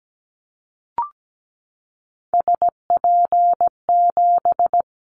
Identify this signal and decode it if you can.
Morse code, heard as a keyed tone, about 17 wpm, 700 Hz. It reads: SP7
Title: Callsign Prefixes for Top 25 DXCC - 17wpm